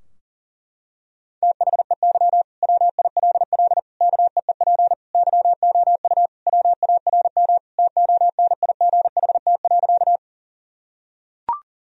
Morse code, heard as a keyed tone, about 37 wpm, 700 Hz. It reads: THEY WILL KEEP YOU WARM TONIGHT.
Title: Sentences from Top 1000 Words - Review Part 3 - 40wpm